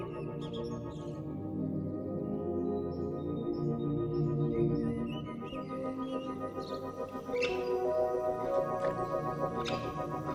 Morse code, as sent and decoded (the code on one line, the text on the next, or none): none